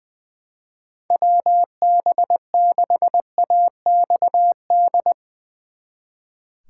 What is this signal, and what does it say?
Morse code, heard as a keyed tone, about 22 wpm, 700 Hz. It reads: WB6AXD
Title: Speed Racing - USA Call Signs 20wpm